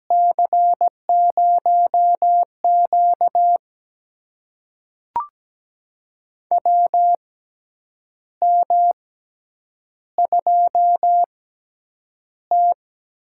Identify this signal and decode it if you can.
Morse code, heard as a keyed tone, about 17 wpm, spaced as for 8 wpm, 700 Hz. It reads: C0Q WM2T
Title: USA Call Signs - 4 Characters - 6x Character Spacing - 17wpm